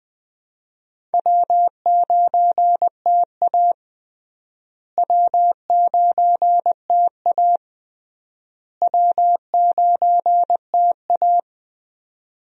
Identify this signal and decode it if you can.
Morse code, heard as a keyed tone, about 20 wpm, 700 Hz. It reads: W9TA W9TA W9TA
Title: USA Call Signs - 4 Characters - Repeated 3x - 20wpm